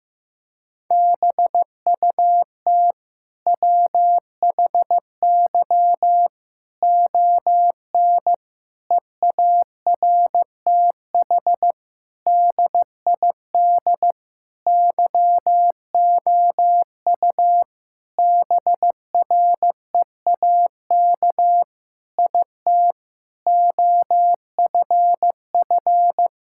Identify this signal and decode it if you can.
Morse code, heard as a keyed tone, about 15 wpm, 700 Hz. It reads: BUT WHY ON EARTH DID YOU BREAK IT OFF
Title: Sentences from Top 1000 Words - Review Part 6 - 15wpm